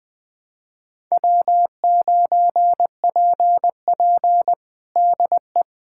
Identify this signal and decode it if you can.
Morse code, heard as a keyed tone, about 20 wpm, 700 Hz. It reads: W9PP DE
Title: USA Callsign DE Callsign - 4 Characters - 20wpm